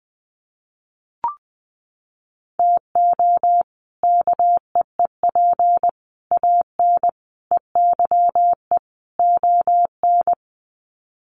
Morse code, heard as a keyed tone, about 20 wpm, 700 Hz. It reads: TO KEEP AN EYE ON
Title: Sets of 5 Words Encore 20wpm